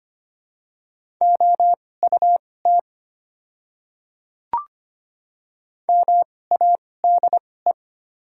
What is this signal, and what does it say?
Morse code, heard as a keyed tone, about 25 wpm, spaced as for 18 wpm, 700 Hz. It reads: OUT MADE
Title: 2x Character Spacing - Top 200 Words - 25wpm